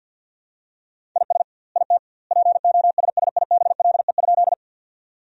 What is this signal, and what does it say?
Morse code, heard as a keyed tone, about 52 wpm, 700 Hz. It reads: IS IT POSSIBLE?
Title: 2x Word-Spacing - Sentences from Top 500 Words - 50wpm